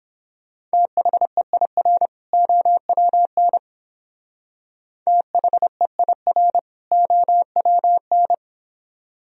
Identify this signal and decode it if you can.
Morse code, heard as a keyed tone, about 30 wpm, 700 Hz. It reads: THEIR OWN THEIR OWN